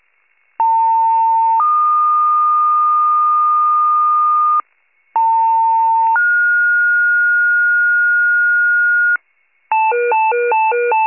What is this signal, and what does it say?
Two-tone page, heard as a one-second tone, then a three-second tone, about 900 then 1220 Hz, about 900 then 1430 Hz, with noise. It is heard from 0.6 to 9.2 s.
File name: wcalert